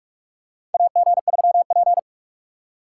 Morse code, heard as a keyed tone, about 47 wpm, 700 Hz. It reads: AG3P
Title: USA Call Signs - 4 Characters - 3x Character Spacing - 45wpm